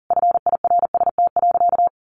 Morse code, heard as a keyed tone, about 40 wpm, 700 Hz. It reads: FIRST.
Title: Sentences from Top 1400 Words 40wpm